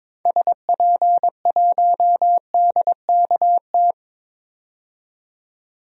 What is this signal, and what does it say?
Morse code, heard as a keyed tone, about 21 wpm, 700 Hz. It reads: SP1DKT
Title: DXCC-100 Call Signs - 6 Characters - 22wpm